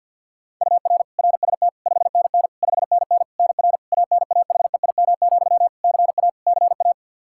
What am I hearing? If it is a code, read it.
Morse code, heard as a keyed tone, about 52 wpm, 700 Hz. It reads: UR RST 5NN 5NN NR ANAHEIM, CA CA
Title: Practice QSO - Exchange 1 - Beginning - 50wpm